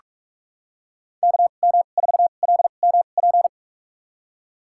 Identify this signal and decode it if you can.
Morse code, heard as a keyed tone, about 45 wpm, spaced as for 32 wpm, 700 Hz. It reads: KM4LMP
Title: USA Call Signs - 6 Characters - 2x Character Spacing - 45wpm